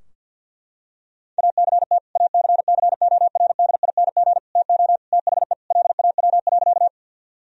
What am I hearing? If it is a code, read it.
Morse code, heard as a keyed tone, about 46 wpm, 700 Hz. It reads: ACT ACCORDING TO THE LAW.
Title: Sentences from Top 1000 Words - Review Part 4 - 50wpm